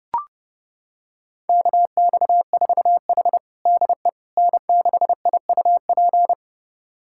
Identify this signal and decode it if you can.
Morse code, heard as a keyed tone, about 30 wpm, 700 Hz. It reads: KX4H DE N6IUP
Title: USA Callsign DE Callsign 30wpm